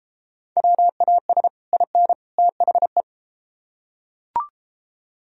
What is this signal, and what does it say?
Morse code, heard as a keyed tone, about 32 wpm, 700 Hz. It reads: WAS IN THE